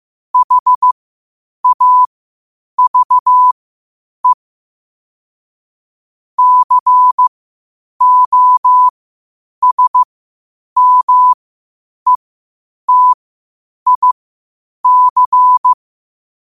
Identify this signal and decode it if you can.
Morse code, heard as a keyed tone, about 15 wpm, spaced as for 8 wpm, 1000 Hz. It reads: HAVE COSMETIC
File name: qotdfiles / 8wpm